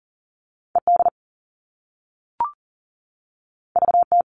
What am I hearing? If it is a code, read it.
Morse code, heard as a keyed tone, about 41 wpm, 700 Hz. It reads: ED VT
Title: All Cut Numbers - 2-Digit 40wpm